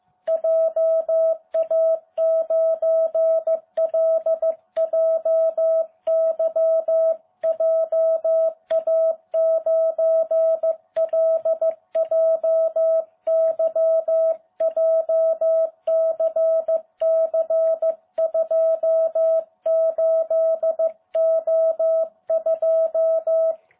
Morse code, heard as a keyed tone, about 15 wpm, 660 Hz. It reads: JA9LJY JA9LJY JCC28O2